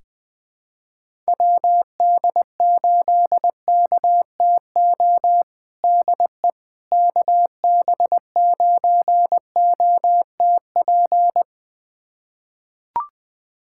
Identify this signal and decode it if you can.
Morse code, heard as a keyed tone, about 20 wpm, 700 Hz. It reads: WD8KTO DE KB9OTP
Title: USA Callsign DE Callsign - 6 Characters - 20wpm